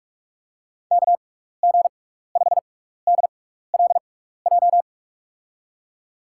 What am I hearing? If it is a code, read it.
Morse code, heard as a keyed tone, about 46 wpm, spaced as for 15 wpm, 700 Hz. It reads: KG5DLJ
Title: USA Call Signs - 6 Characters - 6x Character Spacing - 45wpm